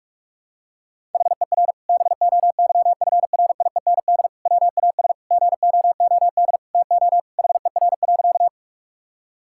Morse code, heard as a keyed tone, about 42 wpm, 700 Hz. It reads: HER BOYFRIEND WAS GOOD TO HER.
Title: Sentences from Top 1000 Words - Review Part 5 - 45wpm